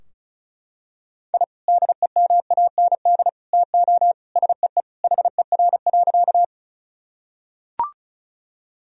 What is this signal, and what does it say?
Morse code, heard as a keyed tone, about 36 wpm, 700 Hz. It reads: I DEMAND TO SEE HER.
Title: Sentences from Top 1000 Words - Review Part 4 - 35wpm